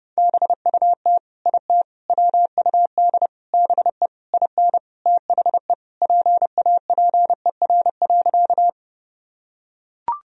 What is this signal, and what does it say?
Morse code, heard as a keyed tone, about 32 wpm, 700 Hz. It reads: BUT IT WUD BE IN THE PAPER.
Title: Sentences from Top 300 Words and Abbreviations 30wpm